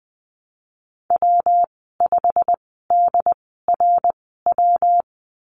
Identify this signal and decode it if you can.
Morse code, heard as a keyed tone, about 20 wpm, spaced as for 15 wpm, 700 Hz. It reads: W5DRW
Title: 173 Lesson 28g - Number 9 - Call Signs 20wpm